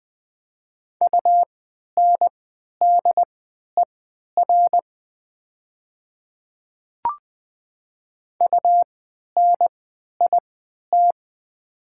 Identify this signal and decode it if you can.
Morse code, heard as a keyed tone, about 20 wpm, spaced as for 11 wpm, 700 Hz. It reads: UNDER UNIT
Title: 3x Character Spacing - Top 500 Words - 20wpm